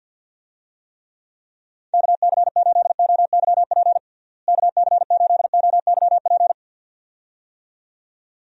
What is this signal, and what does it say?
Morse code, heard as a keyed tone, about 52 wpm, 700 Hz. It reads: KC8OYP KC8OYP
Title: USA Call Signs - 6 Characters - Repeated 2x - 50wpm